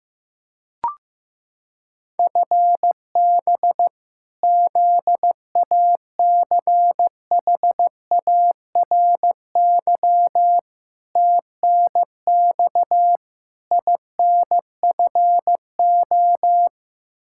Morse code, heard as a keyed tone, about 15 wpm, 700 Hz. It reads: FB ZACHARY TNX INFO
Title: Practice QSO - Exchange 3 - More Info - 15wpm